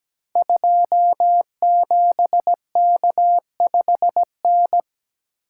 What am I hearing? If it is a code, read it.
Morse code, heard as a keyed tone, about 17 wpm, 700 Hz. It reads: 27K5N
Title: Code Groups - 5 Letter:Numbers - 17wpm